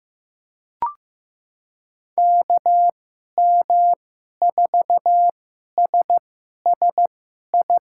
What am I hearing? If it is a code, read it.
Morse code, heard as a keyed tone, about 15 wpm, spaced as for 11 wpm, 700 Hz. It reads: KM4SSI
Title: USA Call Signs - 6 Characters - 2x Character Spacing - 15wpm